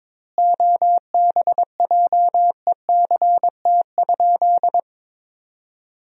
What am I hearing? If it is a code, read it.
Morse code, heard as a keyed tone, about 22 wpm, 700 Hz. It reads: OBJECT?
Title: 4x Word-Spacing - Sentences from Top 500 Words - 22wpm